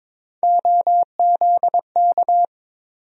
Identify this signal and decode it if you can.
Morse code, heard as a keyed tone, about 22 wpm, 700 Hz. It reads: OZK
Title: USA Call Sign Postfixes - 3 Letters 22wpm